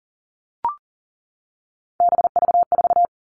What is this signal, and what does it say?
Morse code, heard as a keyed tone, about 40 wpm, 700 Hz. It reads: BV4
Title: All Cut Numbers - 3-Digit 40wpm